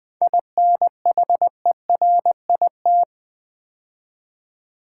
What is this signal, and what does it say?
Morse code, heard as a keyed tone, about 21 wpm, 700 Hz. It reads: INHERIT
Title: 7 Letter Words - First 3 Letters Given - 20wpm